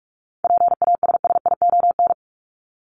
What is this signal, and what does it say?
Morse code, heard as a keyed tone, about 44 wpm, 700 Hz. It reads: PASSION
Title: Top 1001-1250 Words 45wpm